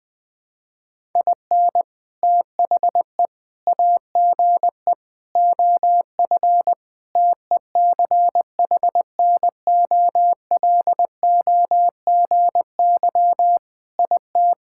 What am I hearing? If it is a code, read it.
Morse code, heard as a keyed tone, about 20 wpm, 700 Hz. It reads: IN THE AGE OF TECHNOLOGY IT
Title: Sentences with 3-Word Phrases - 20wpm